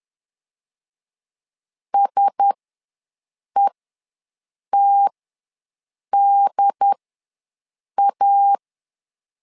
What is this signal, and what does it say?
Morse code, heard as a keyed tone, about 10 wpm, spaced as for 6 wpm, 800 Hz. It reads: SETDA
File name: LEZIONI / 4 Q1S ws 10wpm 2208